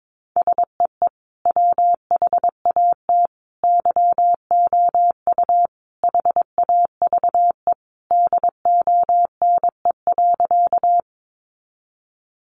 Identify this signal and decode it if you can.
Morse code, heard as a keyed tone, about 22 wpm, 700 Hz. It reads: SEE WHAT YOU HAVE DONE.
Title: Sentences from Top 500 Words 22wpm